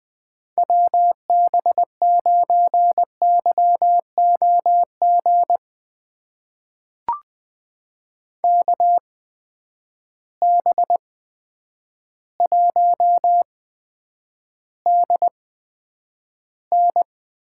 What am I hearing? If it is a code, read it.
Morse code, heard as a keyed tone, about 20 wpm, spaced as for 9 wpm, 700 Hz. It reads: WB9YOG KB1DN
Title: USA Call Signs - 6 Characters - 8x Character Spacing - 20wpm